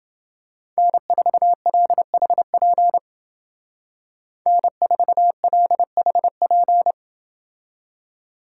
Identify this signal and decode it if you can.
Morse code, heard as a keyed tone, about 29 wpm, 700 Hz. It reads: N4LHP N4LHP